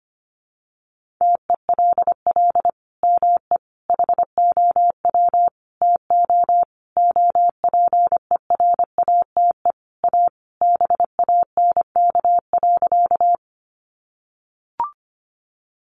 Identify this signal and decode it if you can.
Morse code, heard as a keyed tone, about 25 wpm, 700 Hz. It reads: TELL ME HOW TO OPERATE A BANK.
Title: Sentences from Top 1500 Words 25wpm